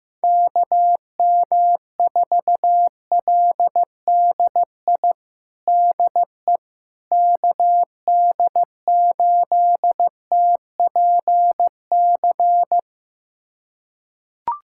Morse code, heard as a keyed tone, about 15 wpm, 700 Hz. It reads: KM4LDI DE KD8TPC
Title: USA Callsign DE Callsign - 6 Characters - 15wpm